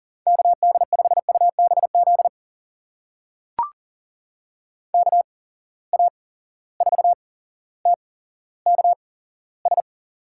Morse code, heard as a keyed tone, about 40 wpm, spaced as for 17 wpm, 700 Hz. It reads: KD5UBZ KA4TKS
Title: USA Call Signs - 6 Characters - 8x Character Spacing - 40wpm